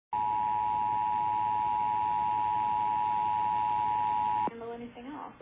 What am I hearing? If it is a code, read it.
Emergency Alert System (SAME) tones (the 853 and 960 Hz attention tone), with speech. It is heard from 0.1 to 4.5 s.